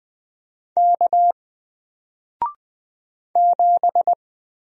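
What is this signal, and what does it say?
Morse code, heard as a keyed tone, about 20 wpm, 700 Hz. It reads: K 7
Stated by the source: Single letter and number Morse drill at 20 wpm, from N4LCD.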